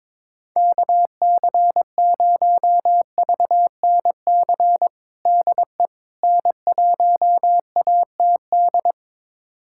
Morse code, heard as a keyed tone, about 22 wpm, 700 Hz. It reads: KC0VNC DE N1ATD
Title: USA Callsign DE Callsign 22wpm